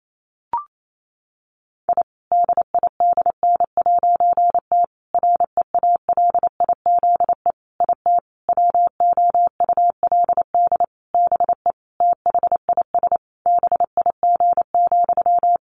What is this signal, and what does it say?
Morse code, heard as a keyed tone, about 28 wpm, 700 Hz. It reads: I DIDN'T REALIZE IT WOULD BE THIS BIG,
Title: A Sound Of Thunder 28wpm